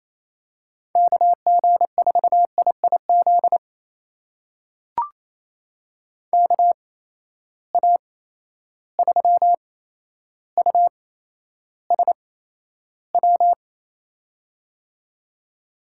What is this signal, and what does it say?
Morse code, heard as a keyed tone, about 28 wpm, spaced as for 12 wpm, 700 Hz. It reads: KG4IIZ KA3USW
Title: USA Call Signs - 6 Characters - 8x Character Spacing - 28wpm